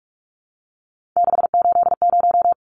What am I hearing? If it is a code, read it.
Morse code, heard as a keyed tone, about 44 wpm, 700 Hz. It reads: 680
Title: Numbers - 3-Digit 45wpm